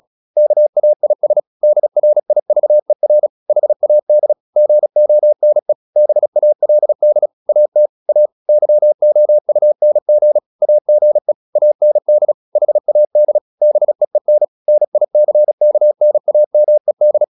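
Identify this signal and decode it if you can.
Morse code, heard as a keyed tone, about 36 wpm, 600 Hz. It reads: KAIS DRIVER HAD GONE BALD AT A YOUNG AGE AND HAD BEEN NICKNAMED
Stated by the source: Morse practice at 35 wpm from CW med Gnister, sending Ken Follett's novel Never in English.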